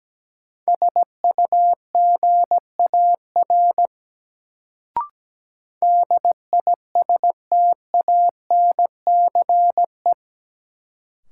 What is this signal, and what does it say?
Morse code, heard as a keyed tone, about 17 wpm, 700 Hz. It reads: SUGAR DISTANCE
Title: Sentences from Top 1200 Words - New Word Drill Alt - 17wpm